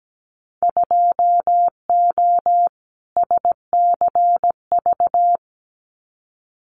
Morse code, heard as a keyed tone, about 17 wpm, 700 Hz. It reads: 2O SCV
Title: Winter Field Day Exchanges Encore 17wpm